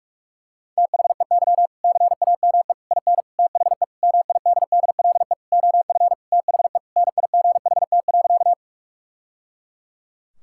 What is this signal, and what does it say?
Morse code, heard as a keyed tone, about 46 wpm, 700 Hz. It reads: THEY CAME IN THE MIDDLE OF THE NIGHT.
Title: Sentences from Top 1000 Words - Review Part 6 - 45wpm